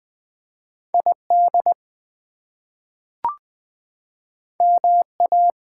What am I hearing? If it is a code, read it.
Morse code, heard as a keyed tone, about 20 wpm, 700 Hz. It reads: ID MA
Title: US States and Canadian Provinces 20wpm